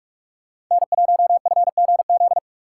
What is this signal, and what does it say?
Morse code, heard as a keyed tone, about 44 wpm, 700 Hz. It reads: N1FGZ